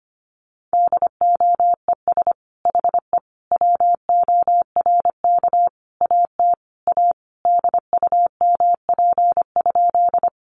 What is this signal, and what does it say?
Morse code, heard as a keyed tone, about 25 wpm, 700 Hz. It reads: DOES HE WORK AT A DUMP?
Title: Sentences from Top 1300 Words 25wpm